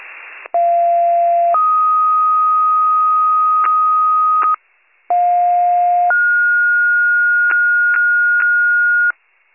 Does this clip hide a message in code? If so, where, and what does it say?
Yes: two-tone page from 0.5 to 9.1 s, heard as a one-second tone, then a three-second tone, about 690 then 1250 Hz, about 710 then 1430 Hz, with noise.